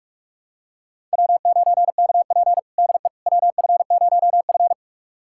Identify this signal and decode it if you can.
Morse code, heard as a keyed tone, about 44 wpm, 700 Hz. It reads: W9KP DE WF0F
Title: USA Callsign DE Callsign - 4 Characters - 45wpm